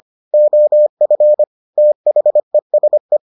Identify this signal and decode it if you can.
Morse code, heard as a keyed tone, about 25 wpm, 600 Hz. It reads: OF THESE